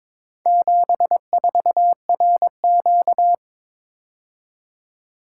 Morse code, heard as a keyed tone, about 23 wpm, 700 Hz. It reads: 74RQ
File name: practice / 22wpm / Four Letter-Number Combinations - 22wpm